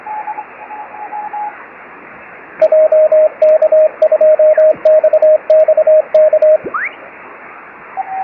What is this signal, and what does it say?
Morse code, heard as a keyed tone, about 26 wpm, 620 Hz, with radio noise. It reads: JK2XXK
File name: FO29 JG8NQJ